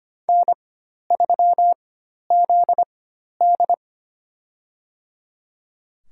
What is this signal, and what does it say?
Morse code, heard as a keyed tone, about 25 wpm, spaced as for 12 wpm, 700 Hz. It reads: N3ZD
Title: USA Call Signs - 4x Character Spacing - 25wpm